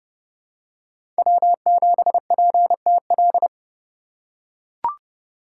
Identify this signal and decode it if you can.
Morse code, heard as a keyed tone, about 29 wpm, 700 Hz. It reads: W7PTL
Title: USA Call Signs - 5 Characters - 2x Character Spacing - 30wpm